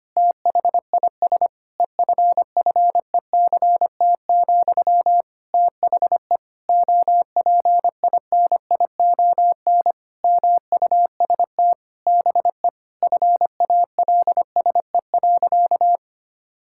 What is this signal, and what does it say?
Morse code, heard as a keyed tone, about 25 wpm, 700 Hz. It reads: THIS EFFECT, THE OPINION MUST BE FALSE.
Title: Sentences from Top 1600 Words 25wpm